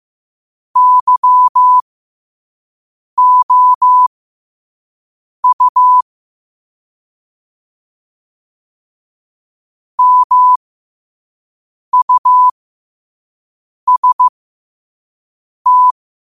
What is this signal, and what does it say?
Morse code, heard as a keyed tone, about 15 wpm, spaced as for 5 wpm, 1000 Hz. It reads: YOU MUST
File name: qotdfiles / 5wpm